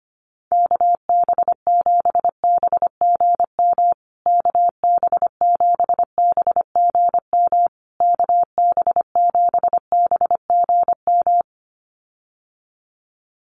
Morse code, heard as a keyed tone, about 25 wpm, 700 Hz. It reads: KB7BGM KB7BGM KB7BGM
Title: USA Call Signs Repeated 3x 25wpm